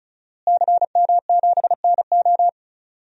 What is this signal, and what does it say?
Morse code, heard as a keyed tone, about 36 wpm, 700 Hz. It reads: CM7NO
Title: DXCC-100 Call Signs - 5 Characters - Repeated 3x - 35wpm